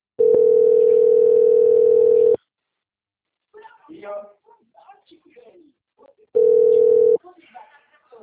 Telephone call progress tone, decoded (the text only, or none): ringback tone